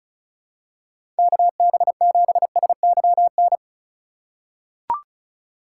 Morse code, heard as a keyed tone, about 36 wpm, 700 Hz. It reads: KB7SYN
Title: USA Call Signs - 6 Characters - 3x Character Spacing - 35wpm